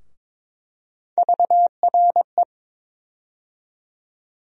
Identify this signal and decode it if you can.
Morse code, heard as a keyed tone, about 22 wpm, 700 Hz. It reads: VRE
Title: USA Call Sign Postfixes - 3 Letters 22wpm